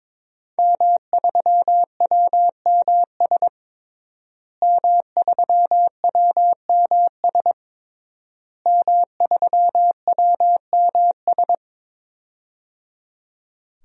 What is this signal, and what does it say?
Morse code, heard as a keyed tone, about 22 wpm, 700 Hz. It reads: M3WMS M3WMS M3WMS
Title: DXCC-100 Call Signs - 5 Characters - Repeated 3x - 22wpm